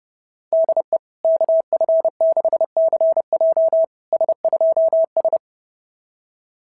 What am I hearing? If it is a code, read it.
Morse code, heard as a keyed tone, about 30 wpm, 650 Hz. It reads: DE KF6CJ S2S
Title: SOTA QSO Practice 30wpm